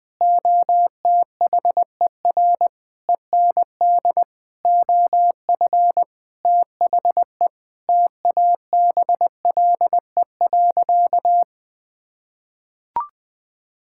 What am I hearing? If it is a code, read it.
Morse code, heard as a keyed tone, about 21 wpm, 700 Hz. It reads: OTHER END OF THE TABLE.
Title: Sentences with 6-Word Phrases - 20wpm